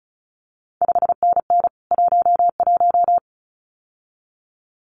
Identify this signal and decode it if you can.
Morse code, heard as a keyed tone, about 36 wpm, 700 Hz. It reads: 5NN 11
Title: CQ WW DX Contest Exchange - 35wpm